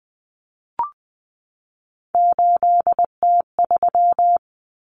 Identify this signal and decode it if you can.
Morse code, heard as a keyed tone, about 20 wpm, 700 Hz. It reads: 8T3
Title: Common Cut Numbers - 3-Digit 20wpm